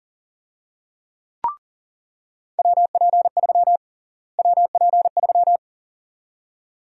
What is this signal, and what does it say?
Morse code, heard as a keyed tone, about 40 wpm, 700 Hz. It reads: WP3 WP3
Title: USA Call Sign Prefixes - Repeated 2x - 40wpm